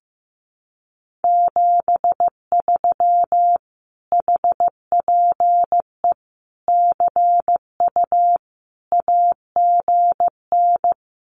Morse code, heard as a keyed tone, about 15 wpm, 700 Hz. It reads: 73 HPE CU AGN
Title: Common QSO Phrases Encore 15wpm